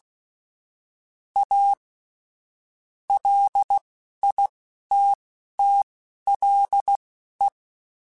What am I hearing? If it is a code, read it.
Morse code, heard as a keyed tone, about 16 wpm, spaced as for 11 wpm, 800 Hz. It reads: A LITTLE
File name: QSOs / QSOs-mp3s / WD0BGP 2@10